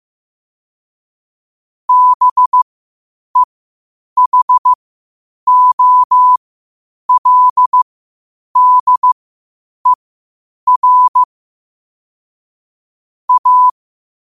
Morse code, heard as a keyed tone, about 15 wpm, spaced as for 8 wpm, 1000 Hz. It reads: BEHOLDER A
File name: qotdfiles / 8wpm